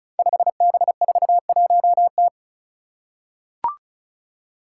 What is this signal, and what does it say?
Morse code, heard as a keyed tone, about 34 wpm, 700 Hz. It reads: 5B41T
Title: Five Challenging Letter-Number Combinations 35wpm